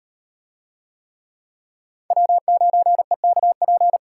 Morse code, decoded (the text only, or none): W9EKP